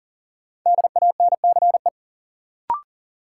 Morse code, heard as a keyed tone, about 40 wpm, 700 Hz. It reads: DANCE
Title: Sentences from Top 700 Words - New Word Drill - 40wpm